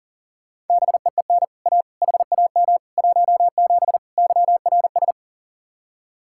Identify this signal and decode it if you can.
Morse code, heard as a keyed tone, about 50 wpm, 700 Hz. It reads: BEEN A HAM 17 YRS
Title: Speed-Racing Common QSO Phrases - 40wpm